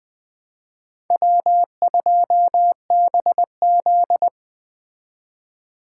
Morse code, heard as a keyed tone, about 25 wpm, 700 Hz. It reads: W2BZ
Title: Speed Racing - USA 1x2 Call Signs 20wpm